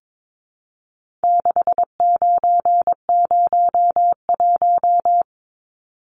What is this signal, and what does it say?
Morse code, heard as a keyed tone, about 22 wpm, 700 Hz. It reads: 6901
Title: Numbers - 4-Digit 22wpm